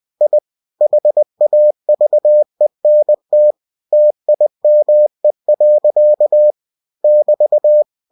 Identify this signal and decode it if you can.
Morse code, heard as a keyed tone, about 20 wpm, 600 Hz. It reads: I HAVENT TIME. <BT>
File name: cw-books / winnie-the-pooh / winnie-20wpm-page-0060